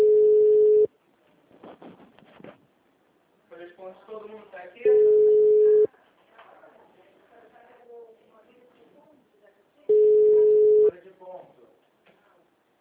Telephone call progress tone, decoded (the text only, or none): ringback tone